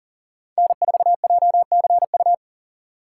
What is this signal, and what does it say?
Morse code, heard as a keyed tone, about 40 wpm, 700 Hz. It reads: N4JCU
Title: USA Call Signs - 5 Characters - 2x Character Spacing - 40wpm